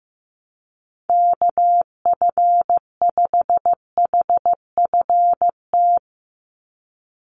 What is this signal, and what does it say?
Morse code, heard as a keyed tone, about 15 wpm, 700 Hz. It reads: KF5HFT
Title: USA Call Signs 15wpm